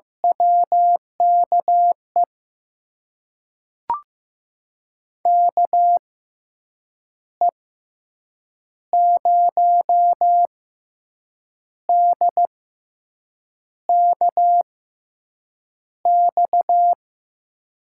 Morse code, heard as a keyed tone, about 15 wpm, spaced as for 6 wpm, 700 Hz. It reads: WKE KE0DKX
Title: USA Call Signs - 6 Characters - 6x Character Spacing - 15wpm